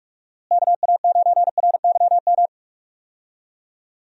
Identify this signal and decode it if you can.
Morse code, heard as a keyed tone, about 45 wpm, 700 Hz. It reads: KA9RYM
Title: USA Call Signs - 6 Characters - Repeated 3x - 45wpm